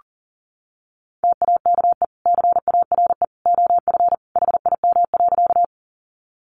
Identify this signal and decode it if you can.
Morse code, heard as a keyed tone, about 40 wpm, 700 Hz. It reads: TAKE CARE OF HIM.
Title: Sentences from Top 300 Words 40wpm